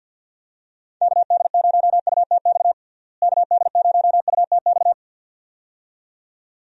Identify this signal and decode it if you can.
Morse code, heard as a keyed tone, about 53 wpm, 700 Hz. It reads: KD0UTX KD0UTX
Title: USA Call Signs - 6 Characters - Repeated 2x - 50wpm